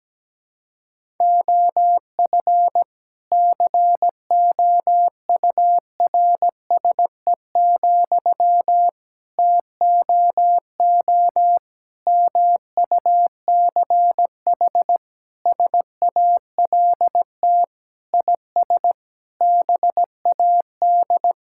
Morse code, heard as a keyed tone, about 17 wpm, 700 Hz. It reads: OF COURSE, TOO MUCH SALT IS BAD
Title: Sentences from Top 1000 Words - Review Part 5 - 17wpm